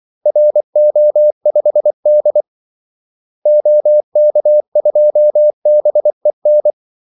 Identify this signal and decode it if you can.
Morse code, heard as a keyed tone, about 24 wpm, 600 Hz. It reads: RO5D OK2BEN
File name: Callsigns-24wpm